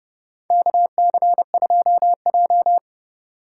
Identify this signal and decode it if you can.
Morse code, heard as a keyed tone, about 32 wpm, 700 Hz. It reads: KC2J
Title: USA Call Signs - 2x Character Spacing - 30wpm